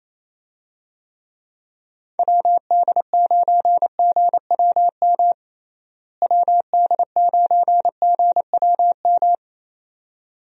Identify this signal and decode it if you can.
Morse code, heard as a keyed tone, about 28 wpm, 700 Hz. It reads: WD9GWM WD9GWM
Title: USA Call Signs - 6 Characters - Repeated 2x - 28wpm